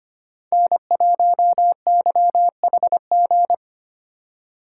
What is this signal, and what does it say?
Morse code, heard as a keyed tone, about 25 wpm, 700 Hz. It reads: N1YHG
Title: USA Call Signs - 5 Characters - 4x Character Spacing - 25wpm